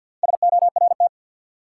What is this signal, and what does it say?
Morse code, heard as a keyed tone, about 47 wpm, 700 Hz. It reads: SORT
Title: Sentences from Top 1000 Words - Review Part 3 - New Word Drill - 50wpm